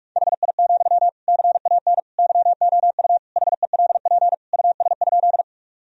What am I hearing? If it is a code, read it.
Morse code, heard as a keyed tone, about 48 wpm, 700 Hz. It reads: HI, CAN YOU HELP US?
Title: Sentences from Top 1000 Words - Review Part 3 - 45wpm